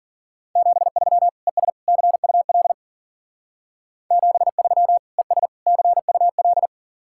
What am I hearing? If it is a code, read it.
Morse code, heard as a keyed tone, about 49 wpm, 700 Hz. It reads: 73 ES CUL 73 ES CUL
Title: Speed-Racing Common QSO Phrases - 40wpm